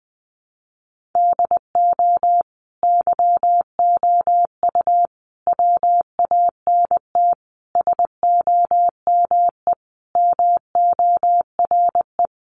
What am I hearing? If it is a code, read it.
Morse code, heard as a keyed tone, about 20 wpm, 700 Hz. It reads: DO YOU WANT SOME MORE
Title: Sentences from Top 300 Words 20wpm